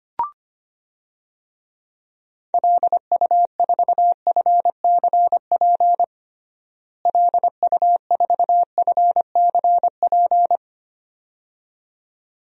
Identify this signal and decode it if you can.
Morse code, heard as a keyed tone, about 25 wpm, 700 Hz. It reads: LU4FCP LU4FCP
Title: DXCC-100 Call Signs - 6 Characters - Repeated 2x - 25wpm